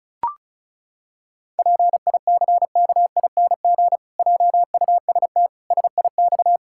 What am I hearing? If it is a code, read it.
Morse code, heard as a keyed tone, about 36 wpm, 700 Hz. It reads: PICKING JUST SIX
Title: Pangram Challenge v2 - 35wpm